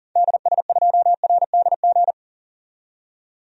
Morse code, heard as a keyed tone, about 38 wpm, 700 Hz. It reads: DS2RDG
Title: DXCC-100 Call Signs - 6 Characters - 40wpm